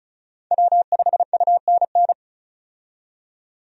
Morse code, heard as a keyed tone, about 36 wpm, 700 Hz. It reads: W5UNN
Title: USA Call Signs - 5 Characters - 35wpm